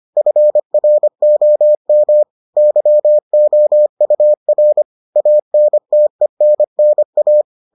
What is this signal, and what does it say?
Morse code, heard as a keyed tone, about 25 wpm, 600 Hz. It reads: FROM YOUR ANTENNA